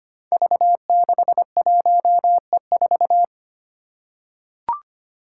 Morse code, heard as a keyed tone, about 25 wpm, 700 Hz. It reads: V61E4
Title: Five Challenging Letter-Number Combinations 25wpm